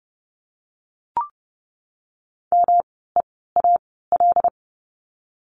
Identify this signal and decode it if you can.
Morse code, heard as a keyed tone, about 30 wpm, spaced as for 17 wpm, 700 Hz. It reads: MEAL
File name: course / 30:10wpm / 108 Lesson 18f - Letter m - Words 30wpm